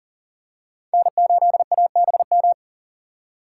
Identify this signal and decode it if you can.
Morse code, heard as a keyed tone, about 40 wpm, 700 Hz. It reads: N8ABM
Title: USA Call Signs - 5 Characters - 4x Character Spacing - 40wpm